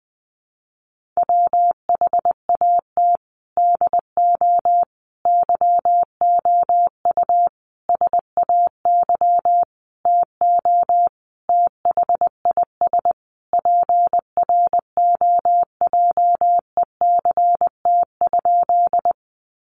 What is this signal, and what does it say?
Morse code, heard as a keyed tone, about 19 wpm, 700 Hz. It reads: WHAT DO YOU SAY TO THIS PROJECT?
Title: Sentences from Top 1400 Words 20wpm